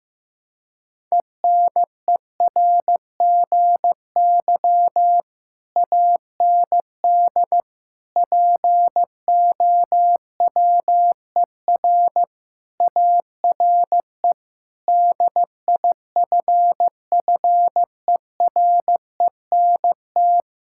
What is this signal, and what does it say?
Morse code, heard as a keyed tone, about 15 wpm, 700 Hz. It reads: ENERGY AND POWER ARE DIFFERENT
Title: Sentences from Top 1000 Words - Review Part 6 - 15wpm